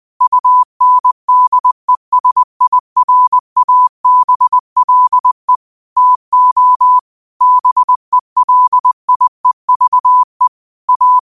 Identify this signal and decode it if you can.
Morse code, heard as a keyed tone, about 20 wpm, 1000 Hz. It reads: UNDESIRABLE TO BELIEVE A
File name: qotdfiles / 20wpm